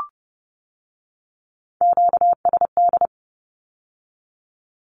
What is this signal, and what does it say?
Morse code, heard as a keyed tone, about 30 wpm, 700 Hz. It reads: QSD
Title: Q-Codes with Spoken Meaning - 30wpm